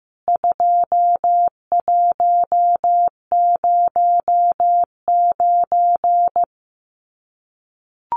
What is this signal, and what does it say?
Morse code, heard as a keyed tone, about 15 wpm, 700 Hz. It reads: 2109
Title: CWT Exchange 15wpm